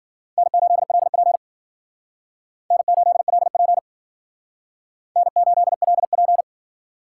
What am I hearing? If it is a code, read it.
Morse code, heard as a keyed tone, about 51 wpm, 700 Hz. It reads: N8LP N8LP N8LP